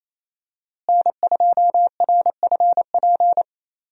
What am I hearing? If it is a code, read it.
Morse code, heard as a keyed tone, about 28 wpm, 700 Hz. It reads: N2RFP
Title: USA Call Signs - 5 Characters - Repeated 3x - 28wpm